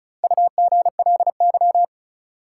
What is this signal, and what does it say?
Morse code, heard as a keyed tone, about 34 wpm, 700 Hz. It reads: UGLY
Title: Four Letter Words - 35wpm